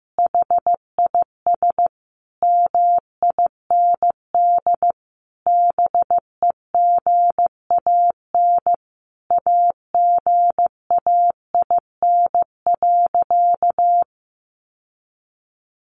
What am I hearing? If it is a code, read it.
Morse code, heard as a keyed tone, about 15 wpm, 700 Hz. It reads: HIS MIND BEGAN AGAIN.
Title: Sentences from Top 1300 Words 15wpm